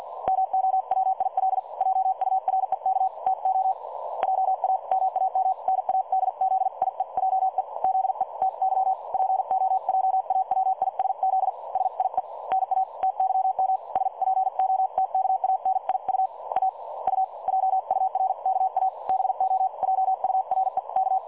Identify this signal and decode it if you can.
Morse code, heard as a keyed tone, about 46 wpm, 760 Hz. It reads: MOWIL JUREK TO JA MAM NADZIEJE ZE TY POJAWISZ SIE NA TYM SPOTKANIU A U JURKA BO JUREK
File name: dlazaawansowanych / nagraniazpasma / websdr recording 3565.0kHz